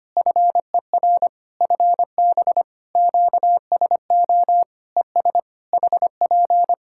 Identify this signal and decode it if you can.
Morse code, heard as a keyed tone, about 25 wpm, 700 Hz. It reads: FER FB QSO ES HP
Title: K1USN SST QSO Practice 25wpm